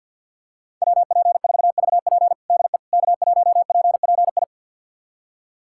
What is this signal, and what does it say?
Morse code, heard as a keyed tone, about 47 wpm, 700 Hz. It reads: WP4VP DE K1PPI
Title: USA Callsign DE Callsign - 5 Characters - 50wpm